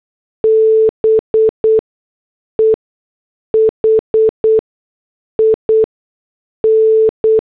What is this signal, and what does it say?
Morse code, heard as a keyed tone, about 8 wpm, spaced as for 6 wpm, 440 Hz. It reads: BEHIN